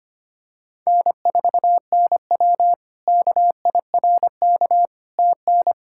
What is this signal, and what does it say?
Morse code, heard as a keyed tone, about 25 wpm, 700 Hz. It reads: N4NW KIRK TN
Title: K1USN SST QSO Practice - 25wpm